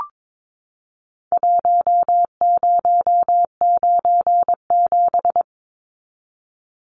Morse code, heard as a keyed tone, about 22 wpm, 700 Hz. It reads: 1097
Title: Numbers - Contest Style - 4-Digit 22wpm